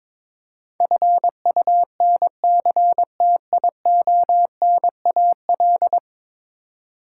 Morse code, heard as a keyed tone, about 22 wpm, 700 Hz. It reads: FUNCTIONAL
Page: morse-code-ninja-practice-22wpm